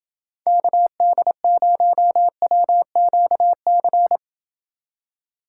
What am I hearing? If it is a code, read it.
Morse code, heard as a keyed tone, about 28 wpm, 700 Hz. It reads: KD0WQC